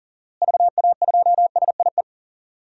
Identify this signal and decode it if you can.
Morse code, heard as a keyed tone, about 40 wpm, 700 Hz. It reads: VA2SIE
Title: DXCC-100 Call Signs - 6 Characters - Repeated 3x - 40wpm